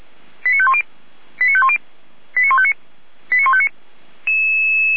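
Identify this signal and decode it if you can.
Five-tone selective call (ZVEI), heard as quick runs of five tones, tones of about 70 ms. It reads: 89510 89510 89159 89159
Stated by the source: ZVEI five-tone selective call received on an SDR, from the Feuerwehr Orlamünde archive.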